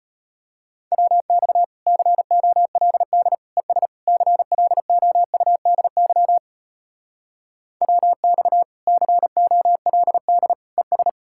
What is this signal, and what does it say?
Morse code, heard as a keyed tone, about 33 wpm, 700 Hz. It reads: WX COLD ES CLOUDY WX COLD ES